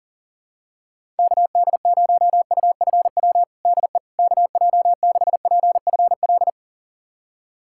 Morse code, heard as a keyed tone, about 37 wpm, 700 Hz. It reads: KD0UFW DE KJ6PFL
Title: USA Callsign DE Callsign - 6 Characters - 40wpm